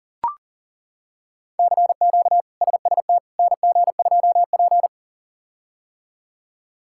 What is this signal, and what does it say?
Morse code, heard as a keyed tone, about 37 wpm, 700 Hz. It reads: CQ SST NG2P
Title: K1USN SST QSO Practice - 40wpm